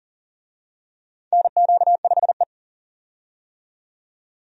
Morse code, heard as a keyed tone, about 40 wpm, 700 Hz. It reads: NQ5E
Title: USA Call Signs - 4 Characters - 40wpm